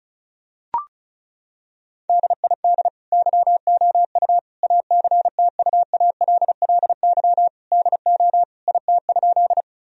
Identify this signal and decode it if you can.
Morse code, heard as a keyed tone, about 36 wpm, 700 Hz. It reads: DID YOU ACTUALLY DO IT?
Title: Sentences from Top 1000 Words - Review Part 4 - 35wpm